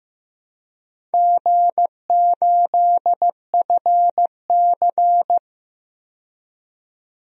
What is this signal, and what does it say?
Morse code, heard as a keyed tone, about 15 wpm, 700 Hz. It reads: G8FC
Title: DXCC-100 Call Signs - 4 Characters - 15wpm